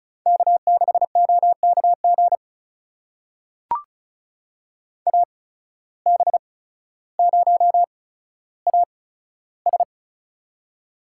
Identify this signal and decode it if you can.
Morse code, heard as a keyed tone, about 36 wpm, spaced as for 15 wpm, 700 Hz. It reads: K6OKG AB0AS
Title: USA Call Signs - 5 Characters - 8x Character Spacing - 35wpm